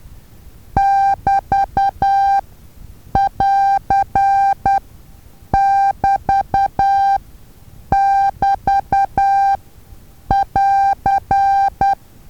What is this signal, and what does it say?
Morse code, heard as a keyed tone, about 10 wpm, spaced as for 7 wpm, 790 Hz. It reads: <BT><AR><BT><BT><AR>
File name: Taviro HA6VA A